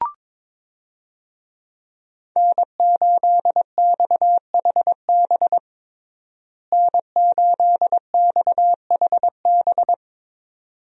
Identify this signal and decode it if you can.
Morse code, heard as a keyed tone, about 22 wpm, 700 Hz. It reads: N8XHB N8XHB
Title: USA Call Signs - 5 Characters - Repeated 2x - 22wpm